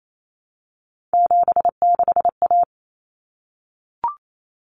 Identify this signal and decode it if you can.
Morse code, heard as a keyed tone, about 28 wpm, 700 Hz. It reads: 76A
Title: Common Cut Numbers - 3-Digit 28wpm